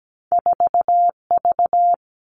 Morse code, heard as a keyed tone, about 17 wpm, 700 Hz. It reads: 4V
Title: Two Challenging Letter-Number Combinations 17wpm